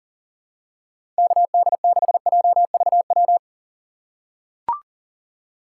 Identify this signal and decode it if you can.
Morse code, heard as a keyed tone, about 38 wpm, 700 Hz. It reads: KD6JVW
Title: USA Call Signs - 6 Characters - 4x Character Spacing - 40wpm